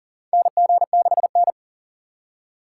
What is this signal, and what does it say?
Morse code, heard as a keyed tone, about 40 wpm, 700 Hz. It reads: NG6N
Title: USA Call Signs - 4 Characters - 6x Character Spacing - 40wpm